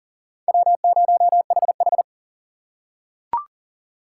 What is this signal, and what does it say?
Morse code, heard as a keyed tone, about 40 wpm, 700 Hz. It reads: W0HH
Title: USA Call Signs - 4 Characters - 2x Character Spacing - 40wpm